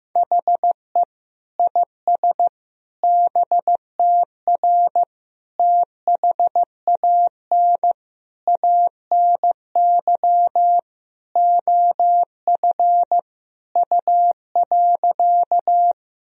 Morse code, heard as a keyed tone, about 15 wpm, 700 Hz. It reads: HE IS BTR THAN ANY OF U.
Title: Sentences from Top 400 Words and Abbreviations 15wpm